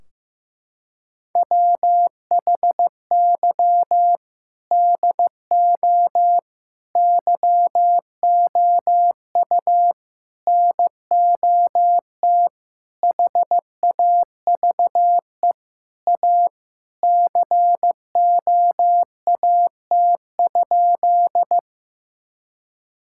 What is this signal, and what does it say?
Morse code, heard as a keyed tone, about 15 wpm, 700 Hz. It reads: WHY DO YOU NOT HAVE A COAT?
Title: Sentences from Top 1600 Words 15wpm